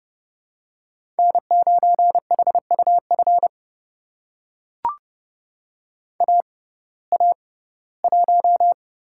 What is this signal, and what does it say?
Morse code, heard as a keyed tone, about 32 wpm, spaced as for 19 wpm, 700 Hz. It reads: N9HUF AA1
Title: USA Call Signs - 5 Characters - 6x Character Spacing - 30wpm